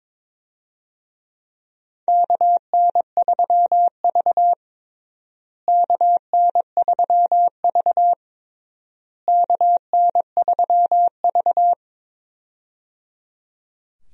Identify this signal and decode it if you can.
Morse code, heard as a keyed tone, about 21 wpm, 700 Hz. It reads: KN3V KN3V KN3V
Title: USA Call Signs - 4 Characters - Repeated 3x - 22wpm